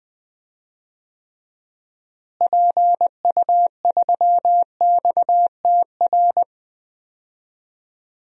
Morse code, heard as a keyed tone, about 20 wpm, 700 Hz. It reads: PU3XTR
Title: DXCC-100 Call Signs - 6 Characters - Repeated 2x - 20wpm